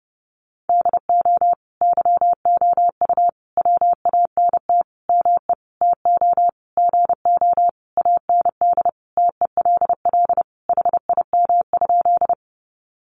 Morse code, heard as a keyed tone, about 30 wpm, 700 Hz. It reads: DO YOU WANT ME TO GO AND TELL HIM?
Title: Sentences from Top 200 Words 30wpm